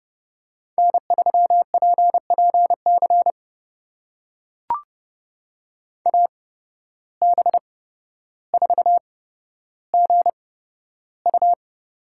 Morse code, heard as a keyed tone, about 32 wpm, spaced as for 13 wpm, 700 Hz. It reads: N3PPC AB4GU
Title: USA Call Signs - 5 Characters - 8x Character Spacing - 30wpm